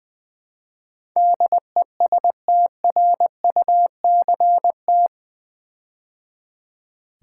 Morse code, heard as a keyed tone, about 20 wpm, 700 Hz. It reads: DESTRUCT
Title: Build A Word - 4 Letters At A Time - 11-Letter Words - 20wpm